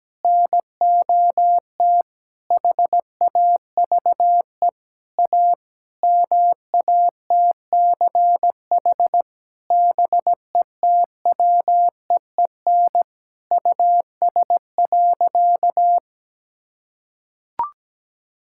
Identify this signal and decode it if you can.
Morse code, heard as a keyed tone, about 17 wpm, 700 Hz. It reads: NOT HAVE A MATCH BETWEEN US.
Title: Sentences from Top 1600 Words 17wpm